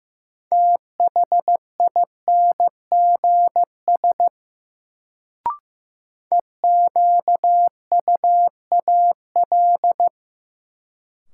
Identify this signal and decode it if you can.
Morse code, heard as a keyed tone, about 15 wpm, 700 Hz. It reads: THINGS EQUAL
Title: Sentences from Top 1400 Words - New Word Drill Alt - 15wpm